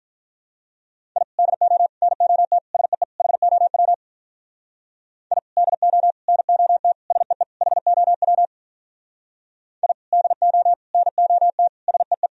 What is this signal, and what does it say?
Morse code, heard as a keyed tone, about 45 wpm, 700 Hz. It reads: I DO NOT SEE HOW I DO NOT SEE HOW I DO NOT SEE